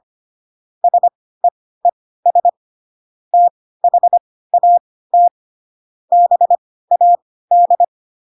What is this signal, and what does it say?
Morse code, heard as a keyed tone, about 25 wpm, spaced as for 16 wpm, 700 Hz. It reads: SEES THAT BAD
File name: Pinocchio / Pinocchio ch4 10 25 ik1hge